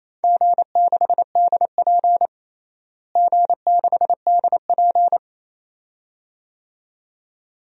Morse code, heard as a keyed tone, about 28 wpm, 700 Hz. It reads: G6DP G6DP
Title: DXCC-100 Call Signs - 4 Characters - Repeated 2x - 28wpm